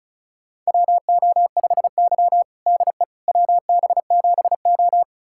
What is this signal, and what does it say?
Morse code, heard as a keyed tone, about 36 wpm, 700 Hz. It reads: WO5Y DE WB7O
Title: USA Callsign DE Callsign - 4 Characters - 35wpm